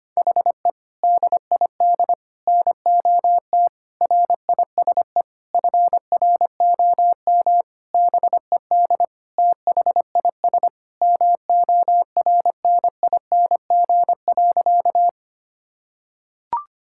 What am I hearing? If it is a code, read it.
Morse code, heard as a keyed tone, about 25 wpm, 700 Hz. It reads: HE DID NOT RISE FROM BED THIS MORNING.
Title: Sentences from Top 1000 Words - Review Part 6 - 25wpm